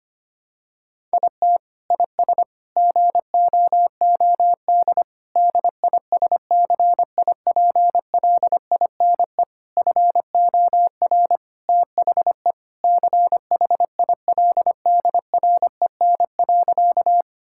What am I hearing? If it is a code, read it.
Morse code, heard as a keyed tone, about 25 wpm, 700 Hz. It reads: IT IS GOOD DISCIPLINE FOR THE CHILDREN.
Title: Sentences from Top 1000 Words - Review Part 6 - 25wpm